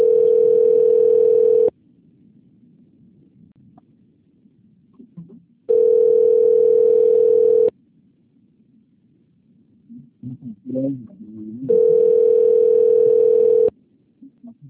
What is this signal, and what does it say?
Ringback tone (North America), 440 + 480 Hz.